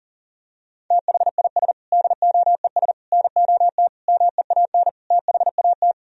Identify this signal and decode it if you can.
Morse code, heard as a keyed tone, about 40 wpm, 700 Hz. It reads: THIS DOES NOT MEAN THAT
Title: Sentences from Top 1000 Words - Review Part 4 - 40wpm